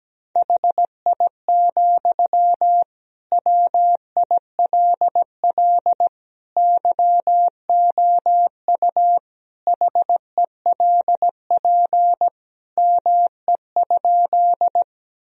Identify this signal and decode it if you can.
Morse code, heard as a keyed tone, about 17 wpm, 700 Hz. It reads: HI, WILL YOU HELP ME?
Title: Sentences from Top 1000 Words - Review Part 3 - 17wpm